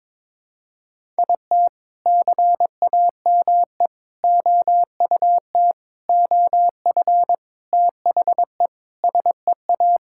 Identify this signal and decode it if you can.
Morse code, heard as a keyed tone, about 22 wpm, 700 Hz. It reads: IT CAME OUT OF THE SEA
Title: Sentences from Top 1000 Words - Review Part 5 - 22wpm